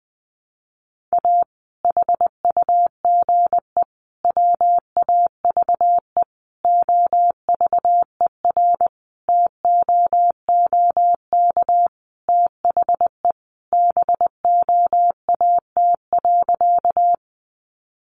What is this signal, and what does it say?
Morse code, heard as a keyed tone, about 20 wpm, 700 Hz. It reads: A HUGE WAVE OVERTOOK THE BOAT.
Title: Sentences from Top 1400 Words 20wpm